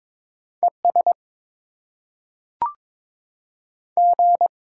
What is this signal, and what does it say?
Morse code, heard as a keyed tone, about 22 wpm, 700 Hz. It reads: E S G
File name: Top 20 QSO Elements - 8x Character Spacing - 22wpm